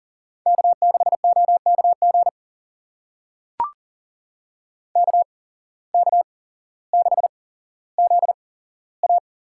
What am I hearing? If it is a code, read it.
Morse code, heard as a keyed tone, about 40 wpm, spaced as for 17 wpm, 700 Hz. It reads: K6OKG KK6ZA
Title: USA Call Signs - 8x Character Spacing - 40wpm